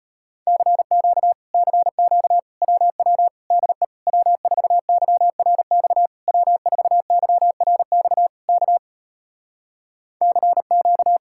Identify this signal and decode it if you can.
Morse code, heard as a keyed tone, about 33 wpm, 700 Hz. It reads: CQ CQ WW DE W4YRX W4YRX K CQ